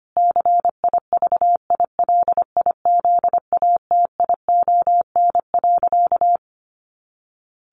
Morse code, heard as a keyed tone, about 25 wpm, 700 Hz. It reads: CIVILIZATION.
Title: A Sound Of Thunder 25wpm